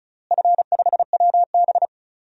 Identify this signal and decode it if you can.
Morse code, heard as a keyed tone, about 36 wpm, 700 Hz. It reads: F5WB
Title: DXCC-100 Call Signs - 4 Characters - Repeated 2x - 35wpm